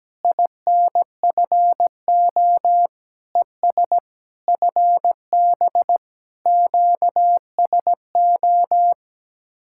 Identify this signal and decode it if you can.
Morse code, heard as a keyed tone, about 17 wpm, 700 Hz. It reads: INFO ES FB QSO
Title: Speed-Racing Common QSO Phrases Encore - 17wpm